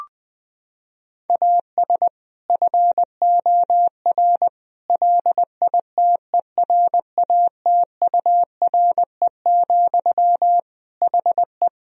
Morse code, heard as a keyed tone, about 20 wpm, 700 Hz. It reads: AS FOR LITERATURE, HE
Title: Sentences from Top 1000 Words - Review Part 3 - 20wpm